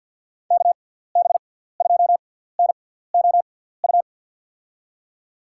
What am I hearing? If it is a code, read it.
Morse code, heard as a keyed tone, about 52 wpm, spaced as for 17 wpm, 700 Hz. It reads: KB2NOU
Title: USA Call Signs - 6 Characters - 6x Character Spacing - 50wpm